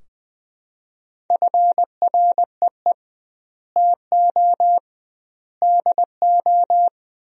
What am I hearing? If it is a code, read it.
Morse code, heard as a keyed tone, about 21 wpm, 700 Hz. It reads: FREE TO DO
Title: 2x Word-Spacing - Sentences from Top 500 Words - 20wpm